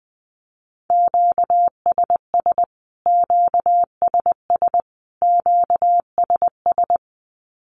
Morse code, heard as a keyed tone, about 20 wpm, 700 Hz. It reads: QSS QSS QSS
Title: Q-Codes Repeated 3x 20wpm